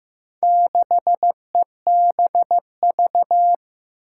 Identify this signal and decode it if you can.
Morse code, heard as a keyed tone, about 15 wpm, 700 Hz. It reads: 6EBV
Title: Four Challenging Letter-Number Combinations 15wpm